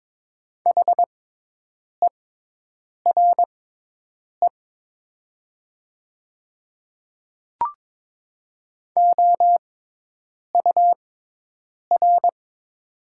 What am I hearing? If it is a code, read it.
Morse code, heard as a keyed tone, about 22 wpm, spaced as for 8 wpm, 700 Hz. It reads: HERE OUR
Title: 6x Character Spacing - Top 200 Words - 22wpm